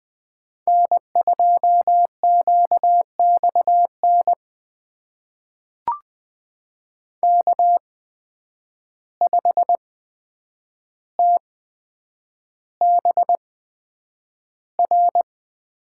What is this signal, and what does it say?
Morse code, heard as a keyed tone, about 20 wpm, spaced as for 9 wpm, 700 Hz. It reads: N2QXN K5TBR
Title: USA Call Signs - 5 Characters - 8x Character Spacing - 20wpm